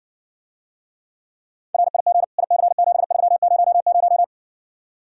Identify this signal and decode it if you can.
Morse code, heard as a keyed tone, about 60 wpm, 700 Hz. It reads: RIG IC7300